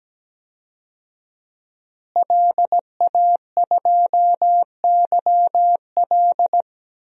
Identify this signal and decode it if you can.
Morse code, heard as a keyed tone, about 17 wpm, 700 Hz. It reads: LA2YL
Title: DXCC-100 Call Signs - 5 Characters - Repeated 2x - 17wpm